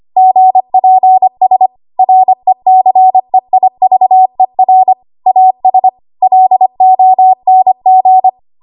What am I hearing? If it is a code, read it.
Morse code, heard as a keyed tone, about 25 wpm, 750 Hz. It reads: GPS RECEIVER AS LONG
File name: Archive / 25 WPM / 25WPM